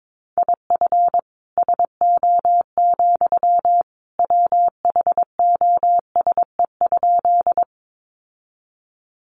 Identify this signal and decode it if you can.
Morse code, heard as a keyed tone, about 22 wpm, 700 Hz. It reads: IF SO, WHOSE?
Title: Sentences from Top 1200 Words 22wpm